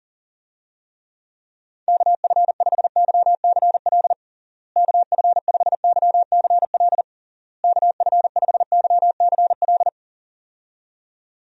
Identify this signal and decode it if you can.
Morse code, heard as a keyed tone, about 40 wpm, 700 Hz. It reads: KF5YCL KF5YCL KF5YCL
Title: USA Call Signs - 6 Characters - Repeated 3x - 40wpm